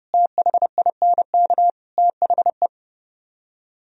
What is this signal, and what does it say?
Morse code, heard as a keyed tone, about 36 wpm, 700 Hz. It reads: THINK THE
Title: Speed-Racing - Sets of 4 Words 30wpm